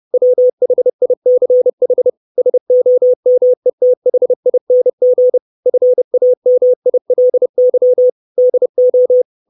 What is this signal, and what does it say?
Morse code, heard as a keyed tone, about 30 wpm, 500 Hz. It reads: WHICH SOMETHING FAMILY DO
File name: 632 Words 30WPM